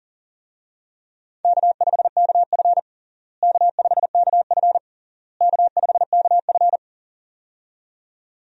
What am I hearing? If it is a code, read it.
Morse code, heard as a keyed tone, about 40 wpm, 700 Hz. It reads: K5KF K5KF K5KF
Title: USA Call Signs - 4 Characters - Repeated 3x - 40wpm